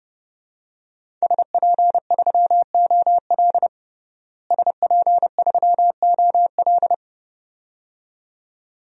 Morse code, heard as a keyed tone, about 30 wpm, 700 Hz. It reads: SP3OL SP3OL
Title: DXCC-100 Call Signs - 5 Characters - Repeated 2x - 30wpm